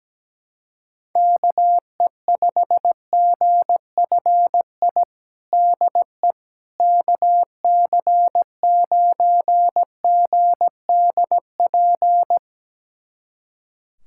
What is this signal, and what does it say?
Morse code, heard as a keyed tone, about 17 wpm, 700 Hz. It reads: KE5GFI DE KC9GDP
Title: USA Callsign DE Callsign 17wpm